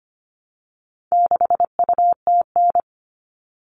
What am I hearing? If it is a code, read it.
Morse code, heard as a keyed tone, about 25 wpm, 700 Hz. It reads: 6UTN
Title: All Cut Numbers - 4-Digit 25wpm